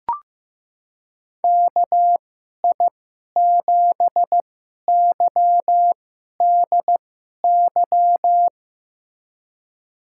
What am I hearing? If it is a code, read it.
Morse code, heard as a keyed tone, about 15 wpm, spaced as for 11 wpm, 700 Hz. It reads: KI7YDY
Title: USA Call Signs - 2x Character Spacing - 15wpm